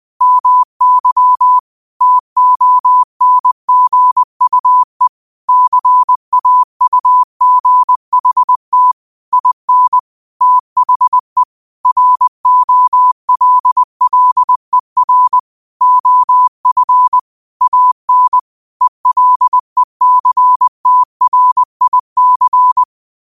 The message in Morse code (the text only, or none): MY TONGUE CAUGHT IN THE ROLLER OF AN ELECTRIC